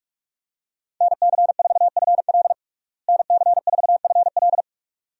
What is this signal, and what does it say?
Morse code, heard as a keyed tone, about 41 wpm, 700 Hz. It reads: NC4FL NC4FL
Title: USA Call Signs - 5 Characters - Repeated 2x - 45wpm